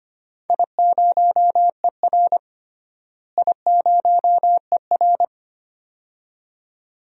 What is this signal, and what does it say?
Morse code, heard as a keyed tone, about 25 wpm, 700 Hz. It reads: I0ER I0ER
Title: DXCC-100 Call Signs - 4 Characters - Repeated 2x - 25wpm